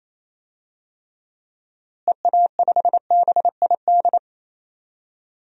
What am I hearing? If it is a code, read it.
Morse code, heard as a keyed tone, about 28 wpm, 700 Hz. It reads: EA5BID
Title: DXCC-100 Call Signs - 6 Characters - Repeated 2x - 28wpm